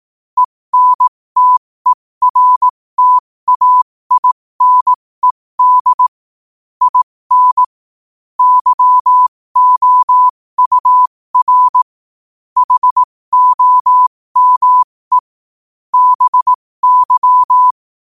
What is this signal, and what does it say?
Morse code, heard as a keyed tone, about 18 wpm, spaced as for 15 wpm, 1000 Hz. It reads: ENTERTAINED IN YOUR HOME BY